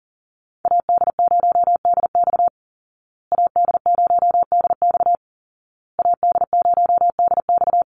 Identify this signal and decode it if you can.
Morse code, heard as a keyed tone, about 40 wpm, 700 Hz. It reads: AD0DX AD0DX AD0DX
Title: Call Signs Repeated 2x - 40wpm